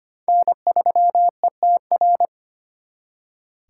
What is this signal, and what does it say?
Morse code, heard as a keyed tone, about 30 wpm, 700 Hz. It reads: N3ETR
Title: Speed Racing - USA 1x3 Call Signs 25wpm